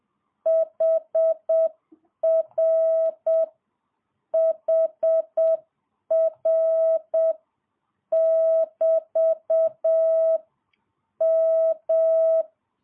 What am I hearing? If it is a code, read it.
Morse code, heard as a keyed tone, about 7 wpm, 650 Hz. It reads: HR HR <BT> M
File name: jh8cbh / 2022cw